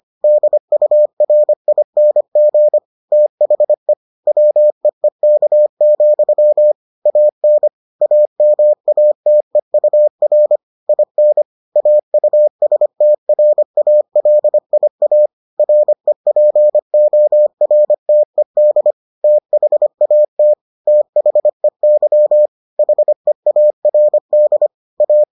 Morse code, heard as a keyed tone, about 25 wpm, 600 Hz. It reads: DURING THE WEEK, AN AMATEUR IN AUSTRALIA REPORTED THAT THEY HEARD A